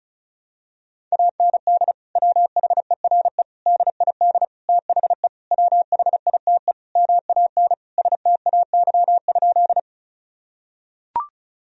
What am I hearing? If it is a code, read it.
Morse code, heard as a keyed tone, about 34 wpm, 700 Hz. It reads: AND WHERE DID THE WHITE MAN STAY?
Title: Sentences from Top 1000 Words - Review Part 1 - 35wpm